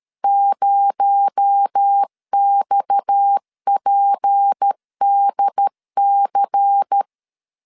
Morse code, heard as a keyed tone, about 13 wpm, 800 Hz. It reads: 0XPDC
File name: CW / DET / 1 9 1500C S A 12wpm 3800